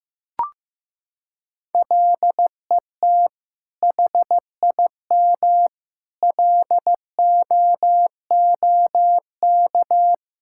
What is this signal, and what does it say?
Morse code, heard as a keyed tone, about 15 wpm, 700 Hz. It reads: LET HIM LOOK
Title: Sentences from Top 1600 Words 15wpm